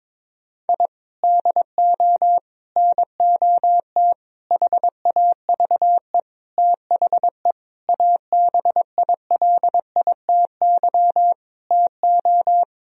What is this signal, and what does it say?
Morse code, heard as a keyed tone, about 22 wpm, 700 Hz. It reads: I DO NOT HAVE THE ABILITY TO
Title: Sentences from Top 1000 Words - Review Part 3 - 22wpm